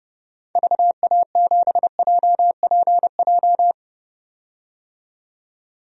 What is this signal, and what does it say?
Morse code, heard as a keyed tone, about 30 wpm, 700 Hz. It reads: VA7JPJ
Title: DXCC-100 Call Signs - 6 Characters - 30wpm